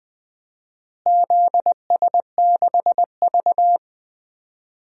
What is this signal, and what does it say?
Morse code, heard as a keyed tone, about 20 wpm, 700 Hz. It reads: ZS6V
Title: DXCC-100 Call Signs - 4 Characters - Repeated 2x - 20wpm